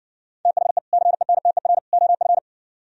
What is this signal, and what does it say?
Morse code, heard as a keyed tone, about 68 wpm, 700 Hz. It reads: THE CENTER OF